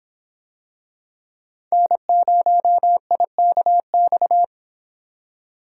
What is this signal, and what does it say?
Morse code, heard as a keyed tone, about 26 wpm, 700 Hz. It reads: N0IKX